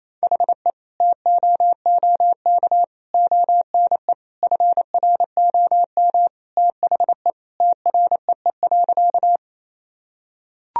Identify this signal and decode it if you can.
Morse code, heard as a keyed tone, about 28 wpm, 700 Hz. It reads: HE TOOK ONE FROM THE TREE.
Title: Sentences from Top 1000 Words - Review Part 1 - 28wpm